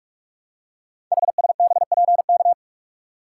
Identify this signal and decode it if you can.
Morse code, heard as a keyed tone, about 44 wpm, 700 Hz. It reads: HSBPK
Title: Code Groups - 5 Letters - 45wpm